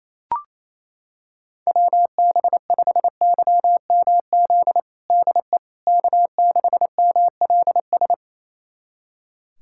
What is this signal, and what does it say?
Morse code, heard as a keyed tone, about 28 wpm, 700 Hz. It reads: WB5YMZ DE K6MLS
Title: USA Callsign DE Callsign 28wpm